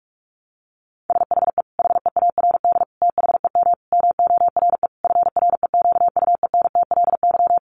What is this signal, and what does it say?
Morse code, heard as a keyed tone, about 44 wpm, 700 Hz. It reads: SHE HEARD THEM MORE FREQUENTLY
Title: Sentences from Top 1300 Words 45wpm